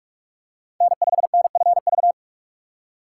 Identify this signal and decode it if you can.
Morse code, heard as a keyed tone, about 41 wpm, 700 Hz. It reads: N5NFV
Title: USA Call Signs - 5 Characters - 2x Character Spacing - 45wpm